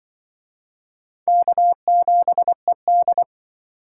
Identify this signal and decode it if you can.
Morse code, heard as a keyed tone, about 25 wpm, 700 Hz. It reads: K7ED
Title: Speed Racing - USA 1x2 Call Signs 20wpm